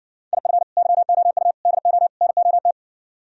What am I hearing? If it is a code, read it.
Morse code, heard as a keyed tone, about 59 wpm, 700 Hz. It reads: IF YOU DO NOT